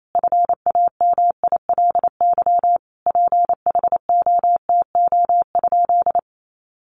Morse code, heard as a keyed tone, about 28 wpm, 700 Hz. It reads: FAMILY PHOTO?
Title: Sentences from Top 1200 Words 28wpm